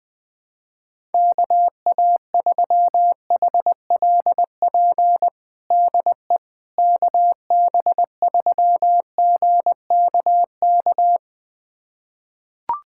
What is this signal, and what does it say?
Morse code, heard as a keyed tone, about 20 wpm, 700 Hz. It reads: KA3HLP DE KB3GKK
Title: USA Callsign DE Callsign 20wpm